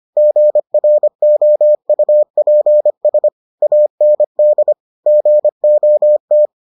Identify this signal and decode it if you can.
Morse code, heard as a keyed tone, about 25 wpm, 600 Hz. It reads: GROUPS AND GOT